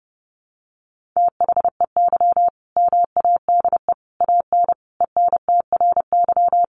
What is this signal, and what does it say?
Morse code, heard as a keyed tone, about 30 wpm, 700 Hz. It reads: THEY MADE AN ENTRY
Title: Sentences from Top 1500 Words 30wpm